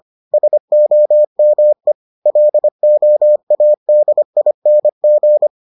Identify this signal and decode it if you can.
Morse code, heard as a keyed tone, about 25 wpm, 600 Hz. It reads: SOME LOADING